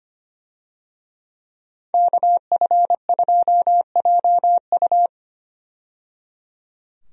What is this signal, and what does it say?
Morse code, heard as a keyed tone, about 25 wpm, 700 Hz. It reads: KF2JU
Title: USA Call Signs - 5 Characters - 25wpm